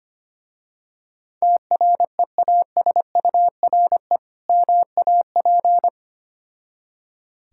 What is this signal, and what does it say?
Morse code, heard as a keyed tone, about 25 wpm, 690 Hz. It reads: TREASURE MAP
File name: practice / 25wpm / Difficult Words with Hints - 25wpm